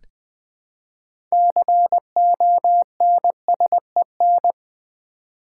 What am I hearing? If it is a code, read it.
Morse code, heard as a keyed tone, about 20 wpm, 700 Hz. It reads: CONSEN
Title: Build A Word - 3 Letters At A Time - 9-Letter Words - 20wpm